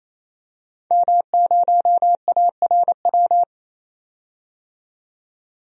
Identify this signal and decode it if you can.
Morse code, heard as a keyed tone, about 28 wpm, 700 Hz. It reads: M0ARW
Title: DXCC-100 Call Signs - 5 Characters - 28wpm